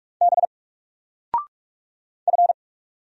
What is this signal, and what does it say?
Morse code, heard as a keyed tone, about 44 wpm, 700 Hz. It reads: B F
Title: Single Letter-Number 45wpm